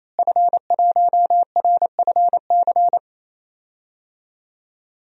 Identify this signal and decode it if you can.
Morse code, heard as a keyed tone, about 28 wpm, 700 Hz. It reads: F1RFC
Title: DXCC-100 Call Signs - 5 Characters - Repeated 3x - 28wpm